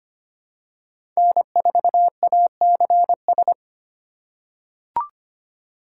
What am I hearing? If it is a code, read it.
Morse code, heard as a keyed tone, about 25 wpm, 700 Hz. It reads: N4ACS
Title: USA Call Signs - 5 Characters - 4x Character Spacing - 25wpm